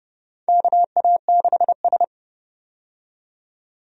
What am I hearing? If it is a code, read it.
Morse code, heard as a keyed tone, about 30 wpm, 700 Hz. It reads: KA6S
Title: USA Call Signs - 4 Characters - 30wpm